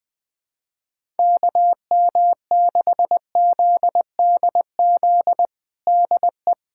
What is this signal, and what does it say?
Morse code, heard as a keyed tone, about 20 wpm, 700 Hz. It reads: KM6ZDZ DE
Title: USA Callsign DE Callsign 20wpm